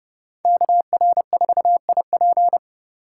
Morse code, heard as a keyed tone, about 28 wpm, 700 Hz. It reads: KR4IP